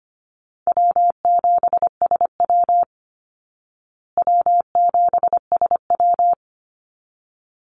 Morse code, heard as a keyed tone, about 25 wpm, 700 Hz. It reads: W7SW W7SW
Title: Easy Call Signs Repeated 2x - 25wpm